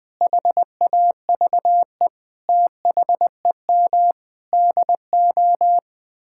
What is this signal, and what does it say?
Morse code, heard as a keyed tone, about 20 wpm, 700 Hz. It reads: HAVE THEM DO
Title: Sentences with 8-Word Phrases - Spoken Beginning - 20wpm